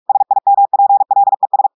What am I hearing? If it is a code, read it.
Morse code, heard as a keyed tone, about 46 wpm, 800 Hz. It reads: SIMPLES
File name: audios / portugais / portugais texte en clair 2 vitesse 45 mots minutes